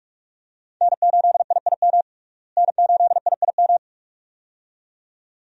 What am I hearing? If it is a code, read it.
Morse code, heard as a keyed tone, about 41 wpm, 700 Hz. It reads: N8IIM N8IIM
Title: USA Call Signs - 5 Characters - Repeated 2x - 45wpm